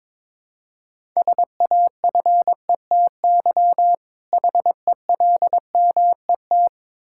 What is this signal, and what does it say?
Morse code, heard as a keyed tone, about 22 wpm, 700 Hz. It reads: SAFETY HELMET
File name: Difficult Words with Hints - 22wpm